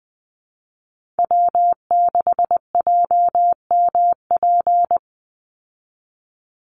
Morse code, heard as a keyed tone, about 20 wpm, 700 Hz. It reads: W6JMP
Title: Call Signs - 20wpm